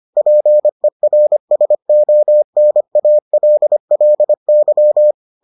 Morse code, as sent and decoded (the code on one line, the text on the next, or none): .--. .-. ... --- -. .- .-.. .-.. -.--
PERSONALLY